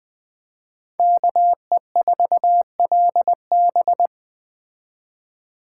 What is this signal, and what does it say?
Morse code, heard as a keyed tone, about 21 wpm, 700 Hz. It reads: KE4LB